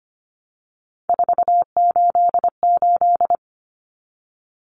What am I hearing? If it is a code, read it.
Morse code, heard as a keyed tone, about 25 wpm, 700 Hz. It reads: 488
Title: Common Cut Numbers - 3-Digit 25wpm